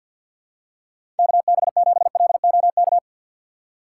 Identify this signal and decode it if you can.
Morse code, heard as a keyed tone, about 47 wpm, 700 Hz. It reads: KB7LOK
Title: USA Call Signs - 2x Character Spacing - 50wpm